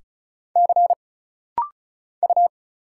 Morse code, heard as a keyed tone, about 36 wpm, 700 Hz. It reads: C U
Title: Single Letters - Rapid-Fire 35wpm